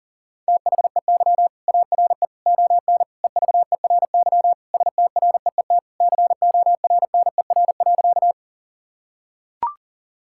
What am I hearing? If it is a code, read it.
Morse code, heard as a keyed tone, about 40 wpm, 700 Hz. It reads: THEY ARE ON EVERY STREET CORNER.
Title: Sentences from Top 1600 Words 40wpm